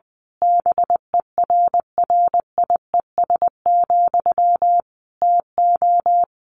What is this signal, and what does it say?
Morse code, heard as a keyed tone, about 20 wpm, 700 Hz. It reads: BERRIES, TO
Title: A Sound Of Thunder 20wpm